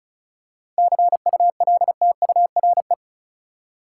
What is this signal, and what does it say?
Morse code, heard as a keyed tone, about 36 wpm, 700 Hz. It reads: CULTURE
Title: Seven Letter Words - 35wpm